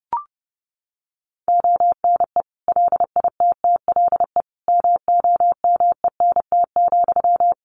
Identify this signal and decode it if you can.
Morse code, heard as a keyed tone, about 30 wpm, 700 Hz. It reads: ONE LITTLE MOMENT,
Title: Sentences from Top 1000 Words 30wpm